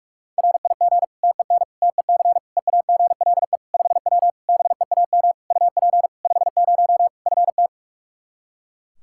Morse code, heard as a keyed tone, about 46 wpm, 700 Hz. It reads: RIG TEN TEC EAGLE 5W BEAM UP 50 FT
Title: Practice QSO - Exchange 2 - Info - 45wpm